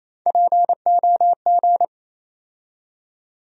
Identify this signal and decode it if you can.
Morse code, heard as a keyed tone, about 28 wpm, 700 Hz. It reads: POG
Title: USA Call Sign Postfixes - 3 Letters 28wpm